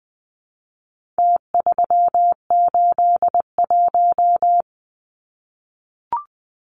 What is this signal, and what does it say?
Morse code, heard as a keyed tone, about 20 wpm, 700 Hz. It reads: T381
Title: Numbers - Contest Style - 4-Digit 20wpm